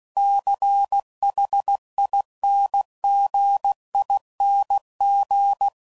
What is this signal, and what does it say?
Morse code, heard as a keyed tone, about 16 wpm, 800 Hz. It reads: CHINGING
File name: Suffixes / Special INGs / Dble Ings